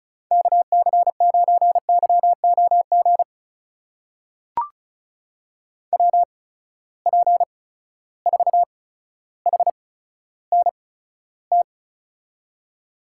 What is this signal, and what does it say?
Morse code, heard as a keyed tone, about 34 wpm, spaced as for 15 wpm, 700 Hz. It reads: KC9YOG WP4HNT
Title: USA Call Signs - 6 Characters - 8x Character Spacing - 35wpm